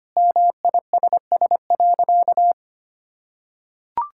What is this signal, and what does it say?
Morse code, heard as a keyed tone, about 25 wpm, 700 Hz. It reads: MISS.
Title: Sentences from Top 1000 Words - Review Part 5 - 25wpm